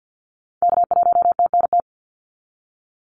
Morse code, heard as a keyed tone, about 50 wpm, 700 Hz. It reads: K1TNT
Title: NA SOTA Activator Callsigns 50wpm